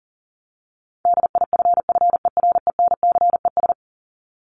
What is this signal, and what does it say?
Morse code, heard as a keyed tone, about 40 wpm, 700 Hz. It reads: DIFFERENCES
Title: ES:S Suffix - Top 100 Words with Base Words 40wpm